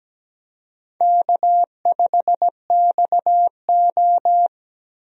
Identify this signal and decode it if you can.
Morse code, heard as a keyed tone, about 17 wpm, 700 Hz. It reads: K5XO
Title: USA Call Signs - 4 Characters - Repeated 2x - 17wpm